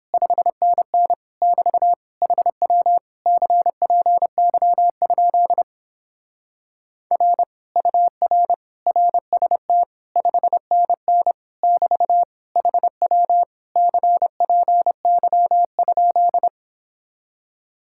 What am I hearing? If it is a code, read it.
Morse code, heard as a keyed tone, about 29 wpm, 700 Hz. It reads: 5NN <BT> HW CPY? R UR RST 5NN <BT> HW CPY?